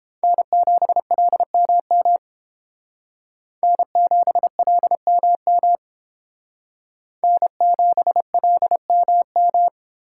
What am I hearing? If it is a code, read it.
Morse code, heard as a keyed tone, about 30 wpm, 700 Hz. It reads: N7LMM N7LMM N7LMM